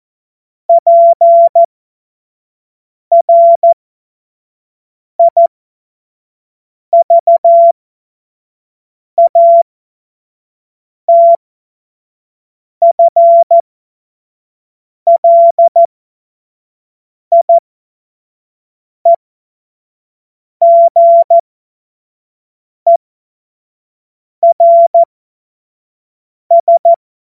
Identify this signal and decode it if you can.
Morse code, heard as a keyed tone, about 14 wpm, spaced as for 5 wpm, 680 Hz. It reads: PRIVATFLIEGERS
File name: Morsekurs / Witze / Witz 20. Juni 2023 25 F70